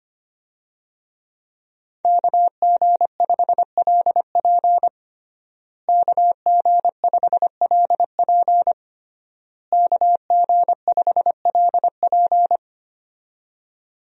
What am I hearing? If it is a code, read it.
Morse code, heard as a keyed tone, about 25 wpm, 700 Hz. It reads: KG5LP KG5LP KG5LP
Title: USA Call Signs - 5 Characters - Repeated 3x - 25wpm